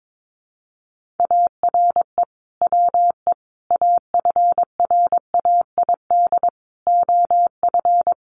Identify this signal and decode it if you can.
Morse code, heard as a keyed tone, about 22 wpm, 700 Hz. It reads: ARE WE AFRAID OF
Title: Sentences from Top 1500 Words 22wpm